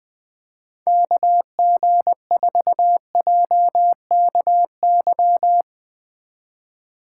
Morse code, heard as a keyed tone, about 19 wpm, 700 Hz. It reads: KG4JKY
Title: Speed Racing - USA 2x3 Call Signs 20wpm